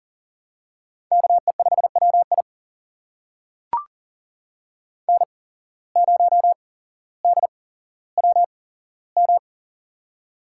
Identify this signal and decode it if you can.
Morse code, heard as a keyed tone, about 40 wpm, spaced as for 17 wpm, 700 Hz. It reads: KE5WI N0DWM
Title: USA Call Signs - 5 Characters - 8x Character Spacing - 40wpm